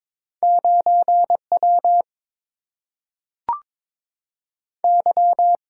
Morse code, heard as a keyed tone, about 22 wpm, 700 Hz. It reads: 9W Y
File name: practice / 22wpm / Two Letter-Number Combinations - 22wpm